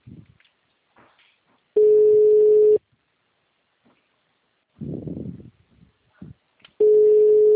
Ringback tone (Europe, 425 Hz).
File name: out- -2004- - - .62972